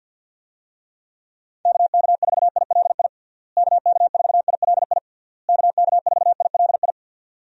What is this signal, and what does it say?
Morse code, heard as a keyed tone, about 52 wpm, 700 Hz. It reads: KK4ILI KK4ILI KK4ILI
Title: USA Call Signs - 6 Characters - Repeated 3x - 50wpm